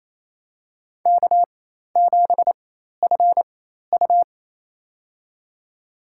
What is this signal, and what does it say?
Morse code, heard as a keyed tone, about 28 wpm, spaced as for 13 wpm, 700 Hz. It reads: K7FU
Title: USA Call Signs - 4 Characters - 4x Character Spacing - 28wpm